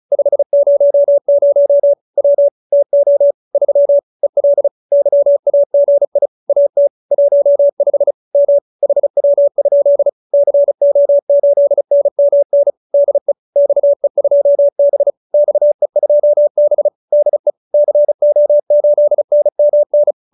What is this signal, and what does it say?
Morse code, heard as a keyed tone, about 34 wpm, 570 Hz. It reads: 500 W TO 3 EL YAGI AT 15 M HW? CO8NMN DE XE2B XE2B DE CO8NMN